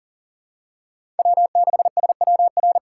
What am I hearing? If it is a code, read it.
Morse code, heard as a keyed tone, about 40 wpm, 700 Hz. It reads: W6SWR
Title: USA Call Signs - 5 Characters - 2x Character Spacing - 40wpm